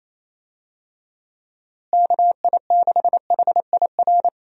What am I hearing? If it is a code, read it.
Morse code, heard as a keyed tone, about 28 wpm, 700 Hz. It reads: KI6HIR
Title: USA Call Signs - 6 Characters - Repeated 3x - 28wpm